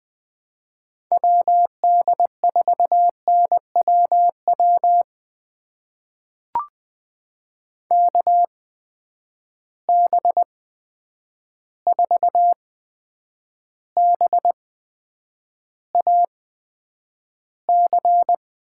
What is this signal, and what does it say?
Morse code, heard as a keyed tone, about 20 wpm, spaced as for 9 wpm, 700 Hz. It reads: WD4NWW KB4BAC
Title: USA Call Signs - 6 Characters - 8x Character Spacing - 20wpm